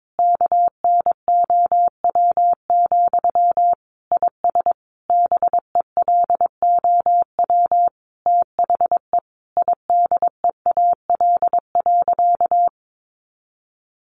Morse code, heard as a keyed tone, about 22 wpm, 700 Hz. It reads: KNOW, IS BELOW THE IDEAL.
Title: Sentences from Top 1400 Words 22wpm